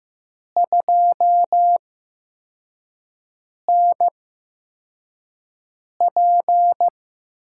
Morse code, heard as a keyed tone, about 15 wpm, spaced as for 4 wpm, 700 Hz. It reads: 2NP
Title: USA Call Signs - 4 Characters - 8x Character Spacing - 15wpm